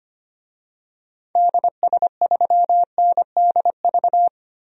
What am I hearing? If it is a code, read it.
Morse code, heard as a keyed tone, about 25 wpm, 700 Hz. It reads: DS3NDV
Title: DXCC-100 Call Signs - 6 Characters - Repeated 2x - 25wpm